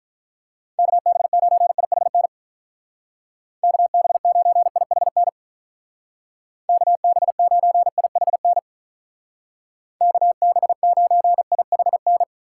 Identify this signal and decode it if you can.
Morse code, heard as a keyed tone, about 47 wpm, 700 Hz. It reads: KB9IHN KB9IHN KB9IHN KB9IHN